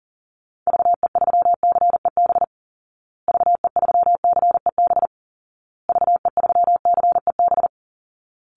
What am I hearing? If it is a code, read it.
Morse code, heard as a keyed tone, about 37 wpm, 700 Hz. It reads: VE3CEB VE3CEB VE3CEB
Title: Call Signs Repeated 2x - 40wpm